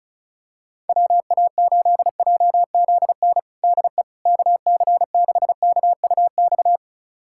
Morse code, heard as a keyed tone, about 36 wpm, 700 Hz. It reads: WA8JZN DE KC6KUX
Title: USA Callsign DE Callsign - 6 Characters - 35wpm